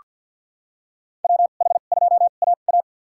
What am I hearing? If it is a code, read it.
Morse code, heard as a keyed tone, about 49 wpm, spaced as for 35 wpm, 700 Hz. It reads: WH2AA
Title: USA Call Signs - 5 Characters - 2x Character Spacing - 50wpm